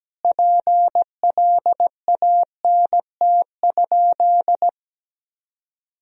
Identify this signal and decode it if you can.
Morse code, heard as a keyed tone, about 17 wpm, 700 Hz. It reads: PLANT?
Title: Sentences from Top 1000 Words - Review Part 1 - 17wpm